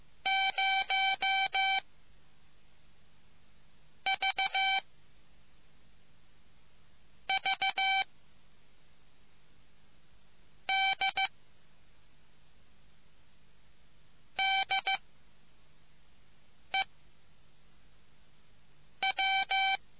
Morse code, heard as a keyed tone, about 15 wpm, 770 Hz. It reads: 0 V V D D E W